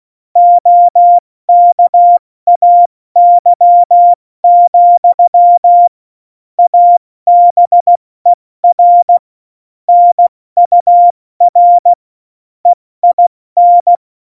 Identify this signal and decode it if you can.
Morse code, heard as a keyed tone, about 16 wpm, 700 Hz. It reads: OKAY, ABER NUR EIN
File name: Morsekurs / Witze / Witz 08. Dezember 2020, 70 F80